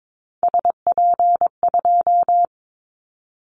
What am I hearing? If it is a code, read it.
Morse code, heard as a keyed tone, about 22 wpm, 700 Hz. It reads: SP2
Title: Callsign Prefixes for Top 25 DXCC - 22wpm